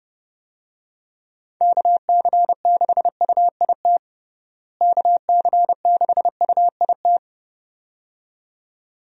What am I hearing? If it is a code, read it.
Morse code, heard as a keyed tone, about 32 wpm, 700 Hz. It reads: KC6UIT KC6UIT
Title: USA Call Signs - 6 Characters - Repeated 2x - 30wpm